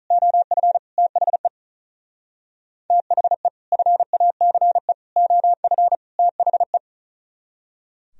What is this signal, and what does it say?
Morse code, heard as a keyed tone, about 44 wpm, 700 Hz. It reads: OF THE THE FACE OF THE
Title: Speed-Racing - Sets of 4 Words 35wpm